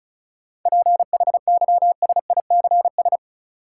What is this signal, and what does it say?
Morse code, heard as a keyed tone, about 36 wpm, 700 Hz. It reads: PHYSICS
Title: Sentences from Top 1500 Words - New Word Drill Alt - 35wpm